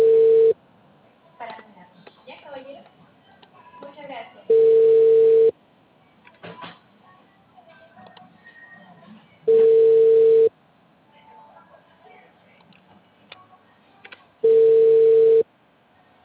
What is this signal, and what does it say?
Ringback tone (China, 450 Hz), with speech.